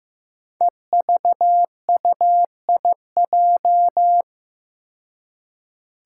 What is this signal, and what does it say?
Morse code, heard as a keyed tone, about 15 wpm, 700 Hz. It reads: EVUIJ
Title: Five Challenging Letter-Number Combinations 15wpm